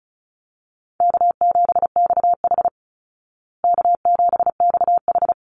Morse code, heard as a keyed tone, about 34 wpm, 700 Hz. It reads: K7XH K7XH
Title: Call Signs Repeated 3x - 35wpm